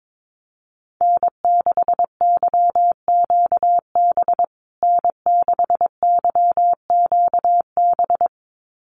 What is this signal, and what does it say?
Morse code, heard as a keyed tone, about 22 wpm, 700 Hz. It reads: N6YQB N6YQB
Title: USA Call Signs Repeated 2x 22wpm